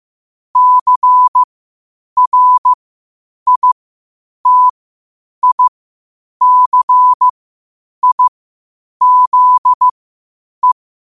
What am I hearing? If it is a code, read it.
Morse code, heard as a keyed tone, about 15 wpm, spaced as for 8 wpm, 1000 Hz. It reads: CRITICIZE
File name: qotdfiles / 8wpm